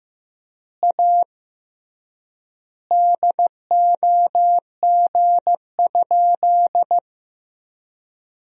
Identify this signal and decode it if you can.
Morse code, heard as a keyed tone, about 15 wpm, 700 Hz. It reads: A DOG?
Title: 3x Word-Spacing - Sentences from Top 400 Words - 15wpm